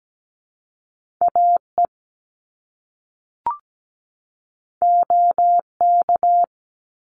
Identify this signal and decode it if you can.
Morse code, heard as a keyed tone, about 17 wpm, 700 Hz. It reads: AE OK
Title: Top Two Letter Combinations in Words 17wpm